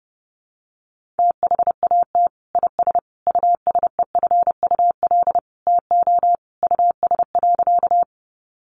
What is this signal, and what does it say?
Morse code, heard as a keyed tone, about 30 wpm, 700 Hz. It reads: THAT IS USEFUL TO US.
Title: Sentences from Top 700 Words 30wpm